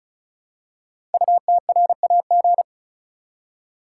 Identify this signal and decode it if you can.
Morse code, heard as a keyed tone, about 36 wpm, 700 Hz. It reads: UTRAG
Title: Code Groups - 5 Letter:Numbers - 35wpm